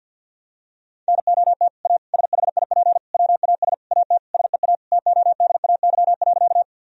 Morse code, heard as a keyed tone, about 52 wpm, 700 Hz. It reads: NOT A SHIP WAS AT SEA TODAY.
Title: Sentences from Top 1000 Words - Review Part 2 - 50wpm